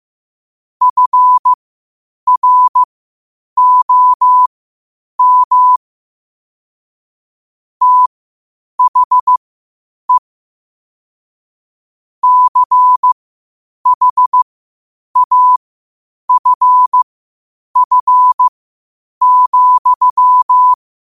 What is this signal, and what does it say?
Morse code, heard as a keyed tone, about 15 wpm, spaced as for 8 wpm, 1000 Hz. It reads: FROM THE CHAFF,
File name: qotdfiles / 8wpm